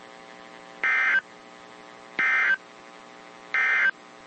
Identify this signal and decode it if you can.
Emergency Alert System (SAME) tones, heard as harsh buzzing bursts. It is heard from 0.7 to 4.0 s.
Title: WZ2523 - NOAA Weather Radio